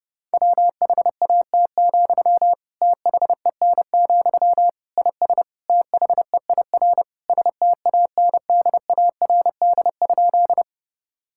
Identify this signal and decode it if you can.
Morse code, heard as a keyed tone, about 30 wpm, 700 Hz. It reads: WHAT, THEN, IS THEIR STANDARD?
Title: Sentences from Top 1000 Words - Review Part 6 - 30wpm